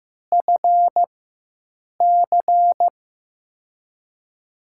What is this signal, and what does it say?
Morse code, heard as a keyed tone, about 15 wpm, spaced as for 7 wpm, 700 Hz. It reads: FC
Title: USA Call Signs - 6 Characters - 4x Character Spacing - 15wpm